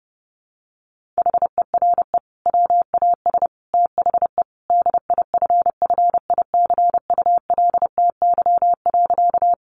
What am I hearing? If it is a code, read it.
Morse code, heard as a keyed tone, about 30 wpm, 700 Hz. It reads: HERE WAS THE DIFFICULTY.
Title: Sentences from Top 1500 Words 30wpm